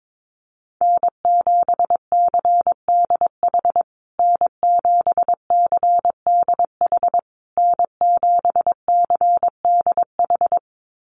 Morse code, heard as a keyed tone, about 22 wpm, 700 Hz. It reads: N7CDH N7CDH N7CDH
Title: USA Call Signs Repeated 3x 22wpm